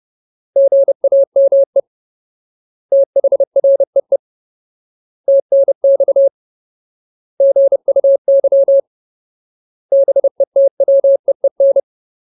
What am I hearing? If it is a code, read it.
Morse code, heard as a keyed tone, about 30 wpm, 550 Hz. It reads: GAME THREE TNX GUY BETWEEN
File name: Ultimate Word List / 30WPM Ultimate